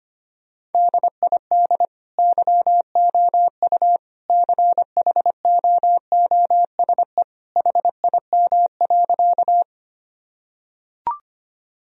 Morse code, heard as a keyed tone, about 25 wpm, 700 Hz. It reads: DID YOU CHOOSE HIM.
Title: Sentences from Top 1000 Words - Review Part 4 - 25wpm